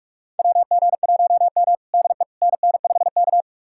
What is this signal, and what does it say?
Morse code, heard as a keyed tone, about 46 wpm, 700 Hz. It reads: WG1M DE NN5K
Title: USA Callsign DE Callsign - 4 Characters - 45wpm